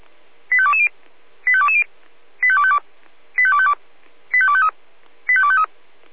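Five-tone selective call (ZVEI), heard as quick runs of five tones, tones of about 70 ms. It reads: 85220 85220 85252 85252 85253 85253
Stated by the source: Received on an SDR; ZVEI five-tone call 85280.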